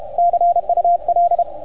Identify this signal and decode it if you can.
Morse code, heard as a keyed tone, about 30 wpm, 690 Hz, with radio noise. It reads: CUL